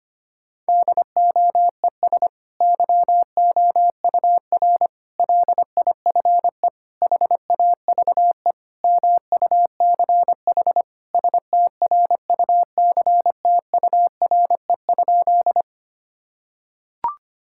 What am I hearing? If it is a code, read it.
Morse code, heard as a keyed tone, about 25 wpm, 700 Hz. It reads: DOES YOUR LIFE HAVE MUCH STRUCTURE?
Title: Sentences from Top 1000 Words - Review Part 6 - 25wpm